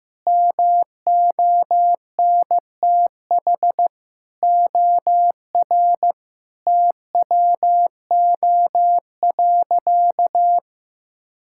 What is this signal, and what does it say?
Morse code, heard as a keyed tone, about 15 wpm, 700 Hz. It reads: MONTH OR TWO.
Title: Sentences from Top 1000 Words - Review Part 5 - 15wpm